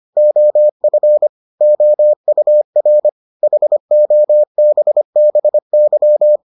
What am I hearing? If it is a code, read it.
Morse code, heard as a keyed tone, about 25 wpm, 600 Hz. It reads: OF OUR HOBBY